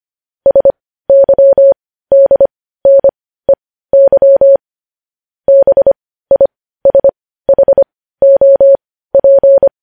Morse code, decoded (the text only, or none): SYDNEY BISHOP